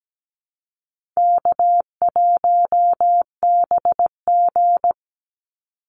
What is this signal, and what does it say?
Morse code, heard as a keyed tone, about 17 wpm, 700 Hz. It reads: K1BG
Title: Easy Call Signs Repeated 3x - 17wpm